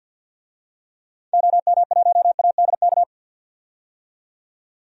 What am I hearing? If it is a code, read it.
Morse code, heard as a keyed tone, about 49 wpm, 700 Hz. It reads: OM1ADK
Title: DXCC-100 Call Signs - 6 Characters - 50wpm